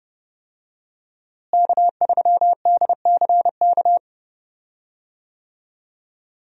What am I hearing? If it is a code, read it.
Morse code, heard as a keyed tone, about 32 wpm, 700 Hz. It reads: K3DCK
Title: USA Call Signs - 5 Characters - Repeated 2x - 30wpm